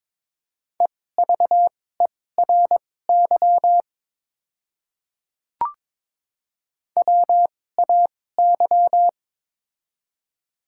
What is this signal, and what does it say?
Morse code, heard as a keyed tone, about 22 wpm, spaced as for 16 wpm, 700 Hz. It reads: EVERY WAY
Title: 2x Character Spacing - Top 300 Words - 22wpm